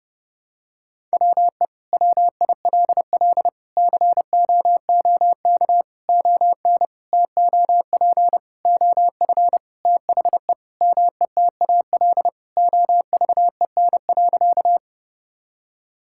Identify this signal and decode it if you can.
Morse code, heard as a keyed tone, about 30 wpm, 700 Hz. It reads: WE WILL COOK ON TOP OF THE METAL OVEN.
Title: Sentences from Top 1000 Words - Review Part 6 - 30wpm